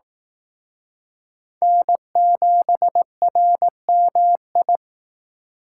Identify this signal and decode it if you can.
Morse code, heard as a keyed tone, about 18 wpm, 700 Hz. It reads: N7RMI
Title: Speed Racing - USA 1x3 Call Signs 15wpm